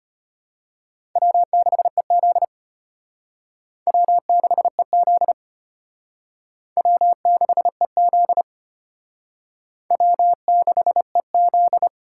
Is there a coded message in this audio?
Yes: Morse code from 1.2 to 11.9 s, heard as a keyed tone, about 32 wpm, 700 Hz. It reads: W6EZ W6EZ W6EZ W6EZ